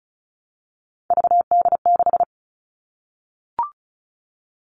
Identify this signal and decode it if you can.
Morse code, heard as a keyed tone, about 34 wpm, 700 Hz. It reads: VD6
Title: All Cut Numbers - 3-Digit 35wpm